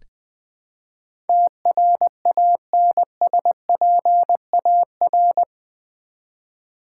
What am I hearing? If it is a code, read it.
Morse code, heard as a keyed tone, about 20 wpm, 700 Hz. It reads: TRANSPAR
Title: Build A Word - 4 Letters At A Time - 12-Letter Words - 20wpm